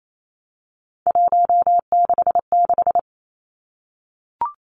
Morse code, heard as a keyed tone, about 28 wpm, 700 Hz. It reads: 166
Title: Three Challenging Letter-Number Combinations 28wpm